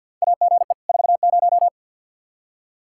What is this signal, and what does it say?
Morse code, heard as a keyed tone, about 52 wpm, 700 Hz. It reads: AGE 40
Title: Practice QSO - Exchange 3 - More Info - 50wpm